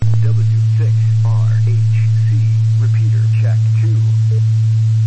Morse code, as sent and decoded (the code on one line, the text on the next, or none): none